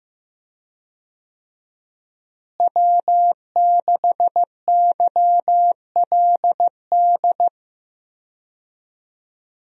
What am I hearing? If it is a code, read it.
Morse code, heard as a keyed tone, about 16 wpm, 700 Hz. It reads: W6YLD
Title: USA Call Signs - 5 Characters - Repeated 3x - 15wpm